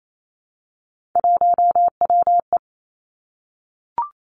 Morse code, heard as a keyed tone, about 28 wpm, 690 Hz. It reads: 1WE
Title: Three Challenging Letter-Number Combinations 28wpm